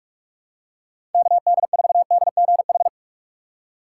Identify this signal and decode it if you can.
Morse code, heard as a keyed tone, about 46 wpm, 700 Hz. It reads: KD4DGH
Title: USA Call Signs - 3x Character Spacing - 45wpm